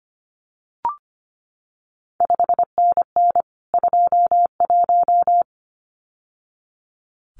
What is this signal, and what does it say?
Morse code, heard as a keyed tone, about 25 wpm, 700 Hz. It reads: 5NN 21
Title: CQ WW DX Contest Exchange Encore - 25wpm